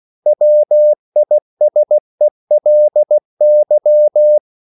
Morse code, heard as a keyed tone, about 16 wpm, 600 Hz. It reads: WISELY
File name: CW / Practice / EnglishQuotations0416